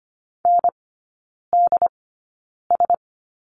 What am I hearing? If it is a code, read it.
Morse code, heard as a keyed tone, about 25 wpm, spaced as for 9 wpm, 700 Hz. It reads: NDS
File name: Lesson 17e - Letter u - Three Characters 25wpm